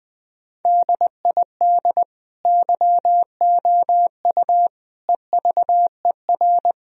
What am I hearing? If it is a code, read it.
Morse code, heard as a keyed tone, about 21 wpm, 700 Hz. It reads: DID YOU EVER
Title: Sentences from Top 1000 Words - Review Part 1 - 20wpm